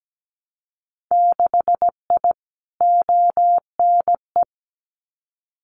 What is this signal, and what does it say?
Morse code, heard as a keyed tone, about 17 wpm, 700 Hz. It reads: 6I ONE
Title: Winter Field Day Exchanges Encore 17wpm